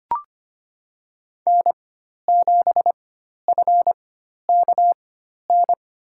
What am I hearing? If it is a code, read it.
Morse code, heard as a keyed tone, about 25 wpm, spaced as for 12 wpm, 700 Hz. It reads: N7FKN
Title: USA Call Signs - 5 Characters - 4x Character Spacing - 25wpm